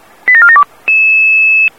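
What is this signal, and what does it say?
Five-tone selective call (ZVEI), heard as quick runs of five tones, tones of about 70 ms. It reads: 87472
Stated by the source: Example fire brigade pager alert tones.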